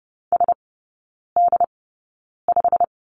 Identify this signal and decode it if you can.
Morse code, heard as a keyed tone, about 30 wpm, spaced as for 9 wpm, 700 Hz. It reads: SD5
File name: Lesson 19e - Letter w - Three Characters 30wpm